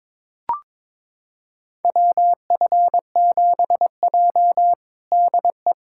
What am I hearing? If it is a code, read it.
Morse code, heard as a keyed tone, about 22 wpm, 700 Hz. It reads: WF7J DE
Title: USA Callsign DE Callsign - 4 Characters - 22wpm